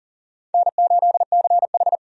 Morse code, heard as a keyed tone, about 40 wpm, 700 Hz. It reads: N8CH
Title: USA Call Signs - 4 Characters - 40wpm